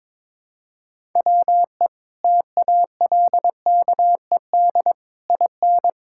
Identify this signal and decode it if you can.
Morse code, heard as a keyed tone, about 22 wpm, 700 Hz. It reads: WE TALKED IN
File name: practice / 22wpm / Sentences with 4-Word Phrases - 22wpm